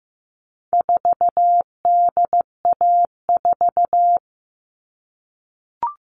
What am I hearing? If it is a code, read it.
Morse code, heard as a keyed tone, about 15 wpm, 700 Hz. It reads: 4DA4
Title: All Cut Numbers - 4-Digit 15wpm